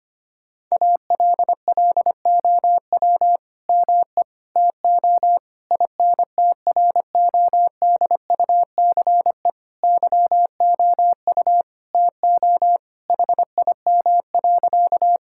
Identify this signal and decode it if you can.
Morse code, heard as a keyed tone, about 25 wpm, 700 Hz. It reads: ALLOW ME TO INTRODUCE YOU TO HIM.
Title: Sentences from Top 1600 Words 25wpm